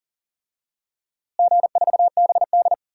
Morse code, heard as a keyed tone, about 40 wpm, 700 Hz. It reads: G4BD
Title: DXCC-100 Call Signs - 4 Characters - Repeated 3x - 40wpm